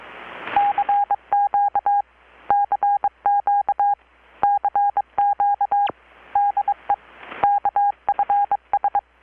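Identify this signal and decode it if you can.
Morse code, heard as a keyed tone, about 23 wpm, 810 Hz. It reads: CQ CQ CQ DE KFS
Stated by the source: Morse from KPH's Night of Nights XX maritime radiotelegraphy event, received in Chiapas, Mexico about 3,800 km away on a Kenwood R-600 with a grounded 10 m long-wire antenna.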